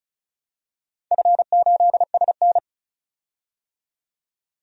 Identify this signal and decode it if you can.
Morse code, heard as a keyed tone, about 36 wpm, 700 Hz. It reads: F8SN
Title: DXCC-100 Call Signs - 4 Characters - 35wpm